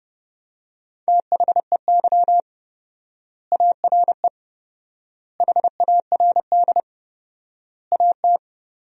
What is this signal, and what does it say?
Morse code, heard as a keyed tone, about 30 wpm, 700 Hz. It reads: THEY ARE HARD AT
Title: 4x Word-Spacing - Sentences from Top 300 Words - 30wpm